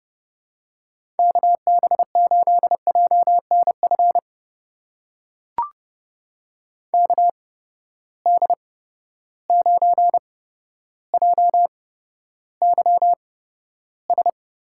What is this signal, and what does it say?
Morse code, heard as a keyed tone, about 31 wpm, spaced as for 13 wpm, 700 Hz. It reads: KB8JNF KD9JYS
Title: USA Call Signs - 6 Characters - 8x Character Spacing - 30wpm